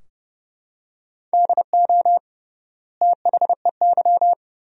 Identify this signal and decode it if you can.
Morse code, heard as a keyed tone, about 30 wpm, 700 Hz. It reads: DO THEY
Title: 3x Word-Spacing - Sentences from Top 100 Words - 30wpm